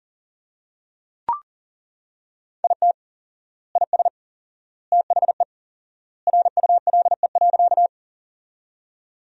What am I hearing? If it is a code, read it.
Morse code, heard as a keyed tone, about 37 wpm, 700 Hz. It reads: IT IS THE RULE.
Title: 4x Word-Spacing - Sentences from Top 400 Words - 40wpm